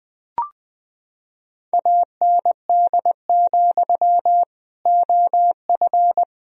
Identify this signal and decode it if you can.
Morse code, heard as a keyed tone, about 20 wpm, 700 Hz. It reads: AND, OF
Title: Sentences from Top 1000 Words - Review Part 1 - 20wpm